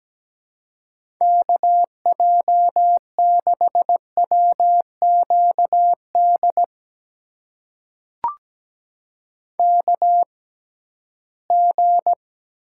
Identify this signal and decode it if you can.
Morse code, heard as a keyed tone, about 17 wpm, spaced as for 13 wpm, 700 Hz. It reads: KJ6WQD KG
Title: USA Call Signs - 6 Characters - 6x Character Spacing - 17wpm